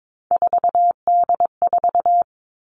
Morse code, heard as a keyed tone, about 22 wpm, 700 Hz. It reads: 4D4
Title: All Cut Numbers - 3-Digit 22wpm